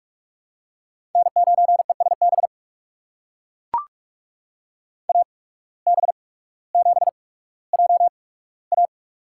Morse code, heard as a keyed tone, about 44 wpm, spaced as for 19 wpm, 700 Hz. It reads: N9ESB AB7JA
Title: USA Call Signs - 5 Characters - 8x Character Spacing - 45wpm